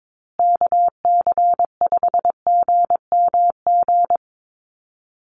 Morse code, heard as a keyed tone, about 22 wpm, 700 Hz. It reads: KC5GMG
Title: USA Call Signs Repeated 3x 22wpm